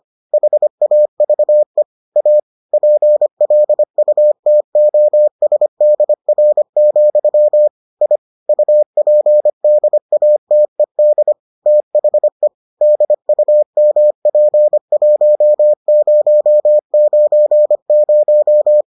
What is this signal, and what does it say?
Morse code, heard as a keyed tone, about 25 wpm, 600 Hz. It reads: HAVE A PLUTOSDR, I UPDATED THE DUMP1090